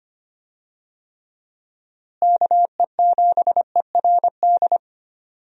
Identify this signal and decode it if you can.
Morse code, heard as a keyed tone, about 25 wpm, 700 Hz. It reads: KE7ERD
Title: USA Call Signs - 6 Characters - Repeated 2x - 25wpm